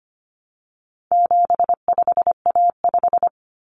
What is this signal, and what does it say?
Morse code, heard as a keyed tone, about 25 wpm, 700 Hz. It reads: 75A5
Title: Common Cut Numbers - 4-Digit 25wpm